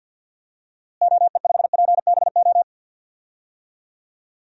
Morse code, heard as a keyed tone, about 51 wpm, 700 Hz. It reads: OE5PBO
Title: DXCC-100 Call Signs - 6 Characters - Repeated 2x - 50wpm